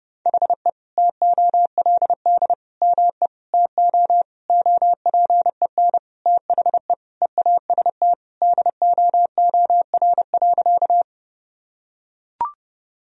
Morse code, heard as a keyed tone, about 30 wpm, 700 Hz. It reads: HE TOLD ME TO OPEN THE EAST DOOR.
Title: Sentences from Top 1000 Words - Review Part 3 - 30wpm